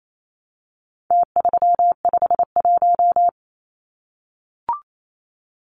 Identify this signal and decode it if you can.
Morse code, heard as a keyed tone, about 28 wpm, 700 Hz. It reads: T351
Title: Numbers - Contest Style - 4-Digit 28wpm